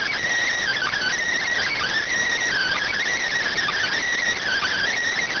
Emergency Alert System (SAME) tones (header bursts), heard as harsh buzzing bursts. It is heard from 0.6 to 4.9 s.